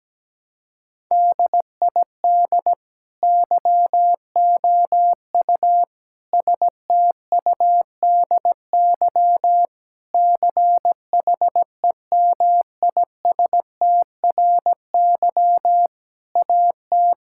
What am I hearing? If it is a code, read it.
Morse code, heard as a keyed tone, about 17 wpm, 700 Hz. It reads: DID YOU STUDY CHEMISTRY AT
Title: Sentences from Top 1000 Words - Review Part 6 - 17wpm